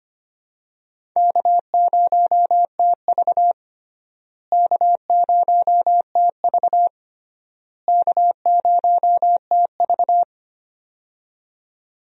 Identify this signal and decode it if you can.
Morse code, heard as a keyed tone, about 25 wpm, 700 Hz. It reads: K0TV K0TV K0TV
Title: USA Call Signs - 4 Characters - Repeated 3x - 25wpm